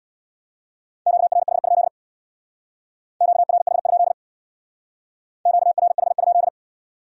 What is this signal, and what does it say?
Morse code, heard as a keyed tone, about 64 wpm, 700 Hz. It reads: QRH? QRH? QRH?